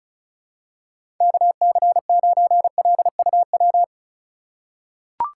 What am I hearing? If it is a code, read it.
Morse code, heard as a keyed tone, about 36 wpm, 700 Hz. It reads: KC9LUW
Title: USA Call Signs - 6 Characters - 3x Character Spacing - 35wpm